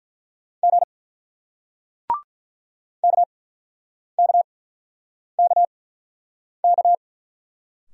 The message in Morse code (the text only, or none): G KKKK